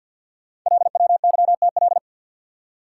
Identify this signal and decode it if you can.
Morse code, heard as a keyed tone, about 50 wpm, 690 Hz. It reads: LWYTL
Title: Code Groups - 5 Letters - 50wpm